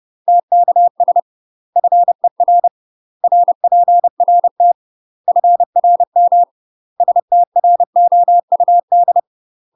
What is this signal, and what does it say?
Morse code, heard as a keyed tone, about 30 wpm, 700 Hz. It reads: TKS FER RPRT FRM STROUD
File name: UK MASTER QSOs TRAINING/30WPM